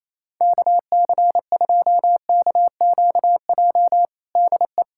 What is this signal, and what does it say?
Morse code, heard as a keyed tone, about 28 wpm, 700 Hz. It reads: KC2KQJ DE
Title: USA Callsign DE Callsign - 6 Characters - 28wpm